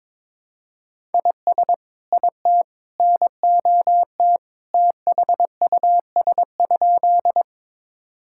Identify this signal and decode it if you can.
Morse code, heard as a keyed tone, about 22 wpm, 700 Hz. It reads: IS IT NOT THUS?
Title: Sentences from Top 1000 Words - Review Part 5 - 22wpm